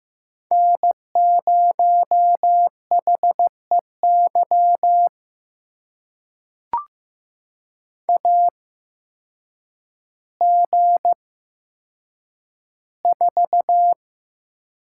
Morse code, heard as a keyed tone, about 15 wpm, spaced as for 8 wpm, 700 Hz. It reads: N0HEY AG4
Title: USA Call Signs - 5 Characters - 8x Character Spacing - 15wpm